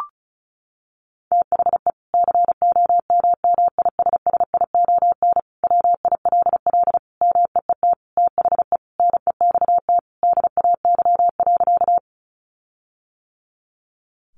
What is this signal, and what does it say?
Morse code, heard as a keyed tone, about 34 wpm, 700 Hz. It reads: THE COMMISSION WILL MEET THE NEXT DAY.
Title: Sentences from Top 1400 Words 35wpm